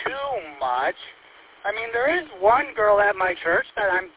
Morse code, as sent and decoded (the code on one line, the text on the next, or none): none